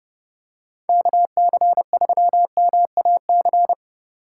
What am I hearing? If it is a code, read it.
Morse code, heard as a keyed tone, about 30 wpm, 700 Hz. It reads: KC3MAC
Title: USA Call Signs - 6 Characters - 2x Character Spacing - 30wpm